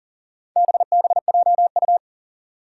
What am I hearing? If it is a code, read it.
Morse code, heard as a keyed tone, about 40 wpm, 700 Hz. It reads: BBJU
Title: Four Challenging Letter-Number Combinations 40wpm